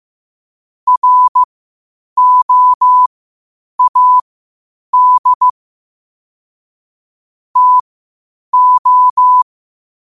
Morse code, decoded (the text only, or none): ROAD TO